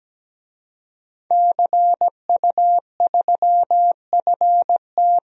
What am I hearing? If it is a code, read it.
Morse code, heard as a keyed tone, about 17 wpm, 700 Hz. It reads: CU3FT
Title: DXCC-100 Call Signs - 5 Characters - Repeated 3x - 17wpm